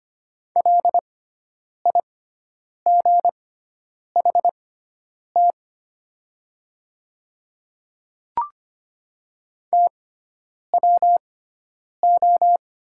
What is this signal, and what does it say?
Morse code, heard as a keyed tone, about 25 wpm, spaced as for 9 wpm, 700 Hz. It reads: LIGHT TWO
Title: 6x Character Spacing - Top 200 Words - 25wpm